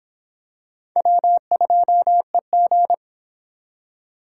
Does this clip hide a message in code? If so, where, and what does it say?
Yes: Morse code from 1.0 to 2.9 s, heard as a keyed tone, about 26 wpm, 700 Hz. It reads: W2EG